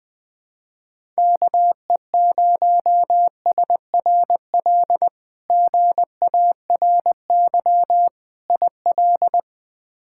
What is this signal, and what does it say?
Morse code, heard as a keyed tone, about 20 wpm, 700 Hz. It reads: KE0SRL GARY IL
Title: K1USN SST QSO Practice - 20wpm